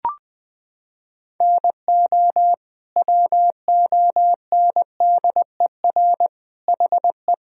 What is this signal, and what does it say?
Morse code, heard as a keyed tone, about 21 wpm, 700 Hz. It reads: NO WONDER HE
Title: Sentences from Top 1000 Words - Review Part 3 - 20wpm